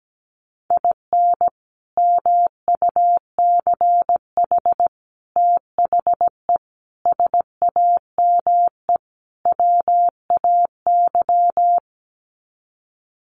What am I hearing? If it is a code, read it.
Morse code, heard as a keyed tone, about 17 wpm, 690 Hz. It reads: IN MUCH THE SAME WAY
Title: Sets of 5 Words Encore 17wpm